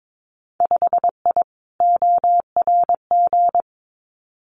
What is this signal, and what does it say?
Morse code, heard as a keyed tone, about 22 wpm, 700 Hz. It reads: 5I ORG
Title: Winter Field Day Exchanges Encore 22wpm